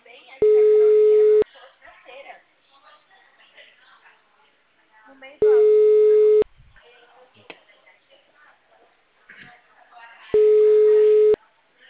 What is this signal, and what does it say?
Ringback tone (Europe, 425 Hz), with speech.